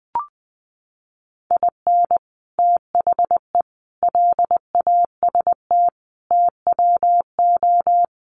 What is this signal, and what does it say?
Morse code, heard as a keyed tone, about 20 wpm, 700 Hz. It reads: IN THE LAST TWO
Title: Sets of 5 Words 20wpm